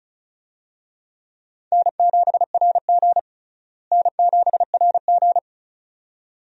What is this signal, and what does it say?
Morse code, heard as a keyed tone, about 34 wpm, 700 Hz. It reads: N7RG N7RG
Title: USA Call Signs - 4 Characters - Repeated 2x - 35wpm